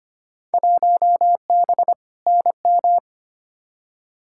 Morse code, heard as a keyed tone, about 25 wpm, 700 Hz. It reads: 1B NM
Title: Field Day Exchanges Encore - 25wpm